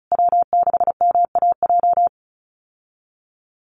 Morse code, heard as a keyed tone, about 34 wpm, 700 Hz. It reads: W6MAJ
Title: USA Call Signs Repeated 3x 35wpm